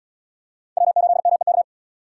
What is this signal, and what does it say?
Morse code, heard as a keyed tone, about 70 wpm, 700 Hz. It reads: W8MEK